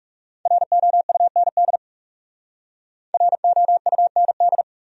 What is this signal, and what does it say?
Morse code, heard as a keyed tone, about 39 wpm, 700 Hz. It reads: ROUND ROUND